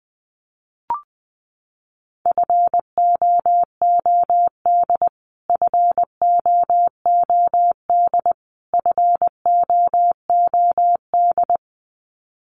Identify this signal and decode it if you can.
Morse code, heard as a keyed tone, about 19 wpm, 700 Hz. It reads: FOOD FOOD FOOD
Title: Top 500 Words Repeated 20wpm